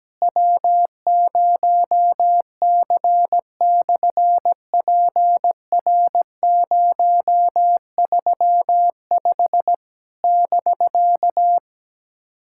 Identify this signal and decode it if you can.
Morse code, heard as a keyed tone, about 17 wpm, 700 Hz. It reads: W0C/PR035 <BK>
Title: SOTA QSO Practice 17wpm